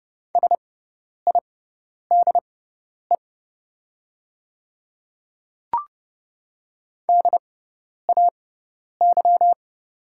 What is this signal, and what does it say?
Morse code, heard as a keyed tone, about 30 wpm, spaced as for 10 wpm, 700 Hz. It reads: SIDE DAY
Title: 6x Character Spacing - Top 100 Words - 30wpm